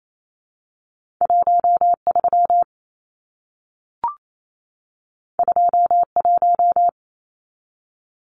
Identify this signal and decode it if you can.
Morse code, heard as a keyed tone, about 28 wpm, 700 Hz. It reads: 13 21
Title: Numbers - 2-Digit 28wpm